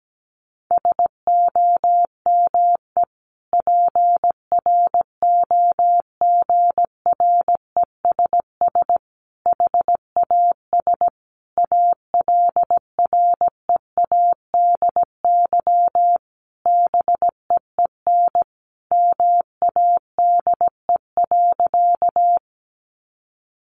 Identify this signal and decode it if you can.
Morse code, heard as a keyed tone, about 17 wpm, 700 Hz. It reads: SOME PROGRESS HAS ALREADY BEEN MADE.
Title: Sentences from Top 1400 Words 17wpm